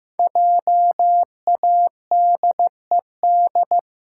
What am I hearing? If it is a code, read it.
Morse code, heard as a keyed tone, about 15 wpm, 700 Hz. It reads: JADED
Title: Pangram Challenge v2 - 15wpm